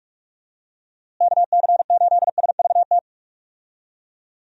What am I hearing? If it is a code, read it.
Morse code, heard as a keyed tone, about 46 wpm, 700 Hz. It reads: KC8SVT
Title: USA Call Signs - 6 Characters - Repeated 3x - 45wpm